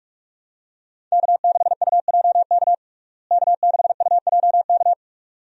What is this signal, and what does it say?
Morse code, heard as a keyed tone, about 46 wpm, 700 Hz. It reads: K6UJK K6UJK
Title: USA Call Signs - 5 Characters - Repeated 2x - 45wpm